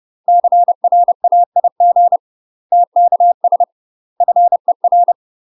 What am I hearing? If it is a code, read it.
Morse code, heard as a keyed tone, about 30 wpm, 700 Hz. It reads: CRAIG TKS FER
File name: UK MASTER QSOs TRAINING / 30WPM / UK MASTER QSOs 0040